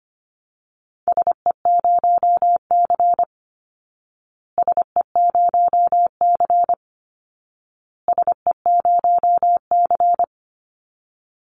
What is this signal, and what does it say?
Morse code, heard as a keyed tone, about 25 wpm, 700 Hz. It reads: SE0C SE0C SE0C
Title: Call Signs Repeated 2x - 25wpm